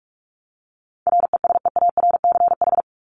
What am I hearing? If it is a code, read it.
Morse code, heard as a keyed tone, about 44 wpm, 700 Hz. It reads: RESEARCH
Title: Top 501-750 Words 45wpm